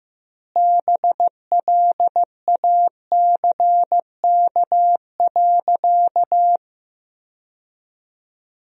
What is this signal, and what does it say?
Morse code, heard as a keyed tone, about 15 wpm, 700 Hz. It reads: BLACK.
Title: 2x Word-Spacing - Sentences from Top 400 Words - 15wpm